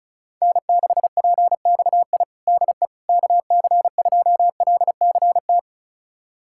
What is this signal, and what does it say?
Morse code, heard as a keyed tone, about 36 wpm, 700 Hz. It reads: N6PXI DE KC2LCT
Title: USA Callsign DE Callsign 35wpm